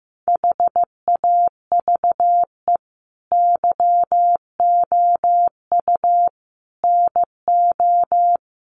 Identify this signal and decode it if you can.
Morse code, heard as a keyed tone, about 15 wpm, 700 Hz. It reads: HAVE YOU NO
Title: Sentences from Top 1500 Words 15wpm